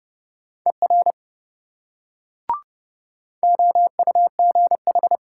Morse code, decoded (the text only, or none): ER OUGH